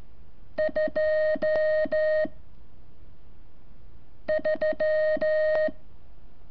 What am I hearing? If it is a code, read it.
Morse code, heard as a keyed tone, about 14 wpm, 620 Hz. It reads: 2 3
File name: curso / ejercicio 28